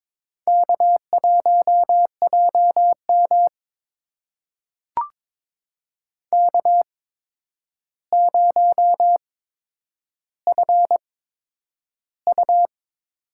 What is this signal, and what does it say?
Morse code, heard as a keyed tone, about 22 wpm, spaced as for 9 wpm, 700 Hz. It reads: K1JM K0FU
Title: USA Call Signs - 4 Characters - 8x Character Spacing - 22wpm